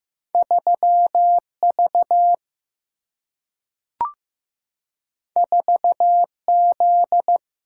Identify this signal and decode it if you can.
Morse code, heard as a keyed tone, about 15 wpm, 700 Hz. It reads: 3V 4Z
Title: Single-Letter Single-Number 15wpm